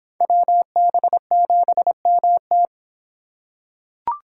Morse code, heard as a keyed tone, about 26 wpm, 700 Hz. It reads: WB7MT